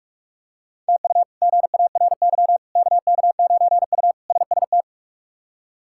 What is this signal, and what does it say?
Morse code, heard as a keyed tone, about 46 wpm, 700 Hz. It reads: TU GARY KK9U SST
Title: K1USN SST QSO Practice - 45wpm